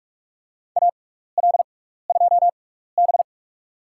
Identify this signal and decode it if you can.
Morse code, heard as a keyed tone, about 44 wpm, spaced as for 15 wpm, 700 Hz. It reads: AL2B
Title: USA Call Signs - 4 Characters - 6x Character Spacing - 45wpm